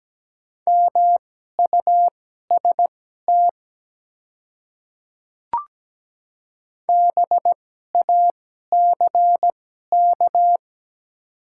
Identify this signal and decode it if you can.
Morse code, heard as a keyed tone, about 17 wpm, spaced as for 12 wpm, 700 Hz. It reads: MUST BACK
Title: 2x Character Spacing - Top 200 Words - 17wpm